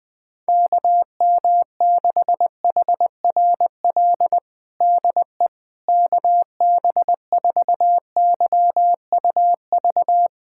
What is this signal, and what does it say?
Morse code, heard as a keyed tone, about 20 wpm, 700 Hz. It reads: KM6HRL DE KB4YUV
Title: USA Callsign DE Callsign 20wpm